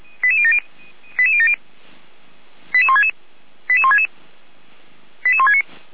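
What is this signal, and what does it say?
Five-tone selective call (ZVEI), heard as quick runs of five tones, tones of about 70 ms. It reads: 88070 88166 88166 88169
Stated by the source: ZVEI five-tone selective call 88192, received by SDR, from feuerwehr-orlamuende.org (Feuerwehr Orlamünde).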